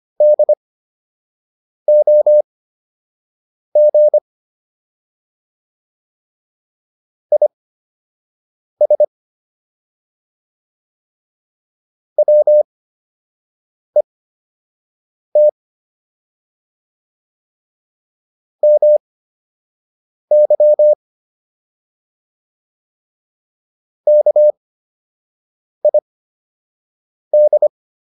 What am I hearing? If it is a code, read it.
Morse code, heard as a keyed tone, about 25 wpm, spaced as for 6 wpm, 600 Hz. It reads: DOG IS WET MY KID